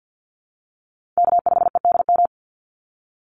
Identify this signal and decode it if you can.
Morse code, heard as a keyed tone, about 50 wpm, 700 Hz. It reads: K5EDM
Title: Call Signs - 50wpm